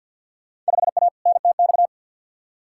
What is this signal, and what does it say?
Morse code, heard as a keyed tone, about 53 wpm, 700 Hz. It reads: 5A NTX
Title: Field Day Exchanges - 50wpm